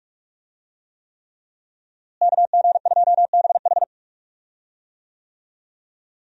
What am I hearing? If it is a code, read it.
Morse code, heard as a keyed tone, about 48 wpm, 700 Hz. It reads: KG2BH
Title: USA Call Signs - 5 Characters - Repeated 3x - 45wpm